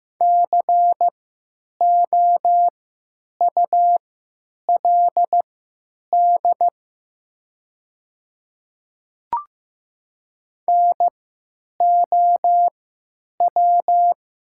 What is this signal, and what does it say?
Morse code, heard as a keyed tone, about 15 wpm, spaced as for 9 wpm, 700 Hz. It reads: COULD NOW
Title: 3x Character Spacing - Top 100 Words - 15wpm